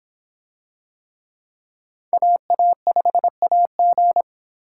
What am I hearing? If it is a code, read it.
Morse code, heard as a keyed tone, about 26 wpm, 700 Hz. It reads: AA5AG